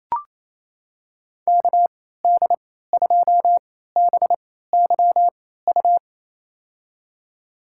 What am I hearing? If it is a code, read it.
Morse code, heard as a keyed tone, about 28 wpm, spaced as for 16 wpm, 700 Hz. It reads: KD2BYU
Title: USA Call Signs - 6 Characters - 3x Character Spacing - 28wpm